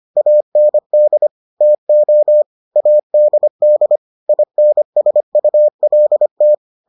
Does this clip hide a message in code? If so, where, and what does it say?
Yes: Morse code from 0.2 to 6.5 s, heard as a keyed tone, about 25 wpm, 600 Hz. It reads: AND TO ADD INSULT